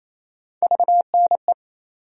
Morse code, heard as a keyed tone, about 28 wpm, 700 Hz. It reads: VNE
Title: USA Call Sign Postfixes - 3 Letters 28wpm